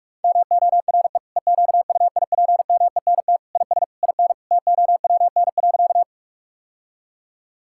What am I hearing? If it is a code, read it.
Morse code, heard as a keyed tone, about 46 wpm, 700 Hz. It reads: MORE EQUIPMENT IS IN TOWN.
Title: Sentences from Top 1000 Words - Review Part 5 - 45wpm